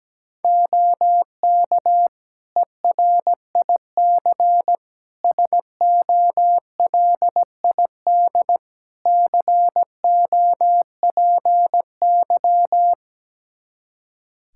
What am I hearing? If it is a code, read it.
Morse code, heard as a keyed tone, about 17 wpm, 700 Hz. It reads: OK ERIC SOLID COPY
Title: Practice QSO - Exchange 3 - More Info - 17wpm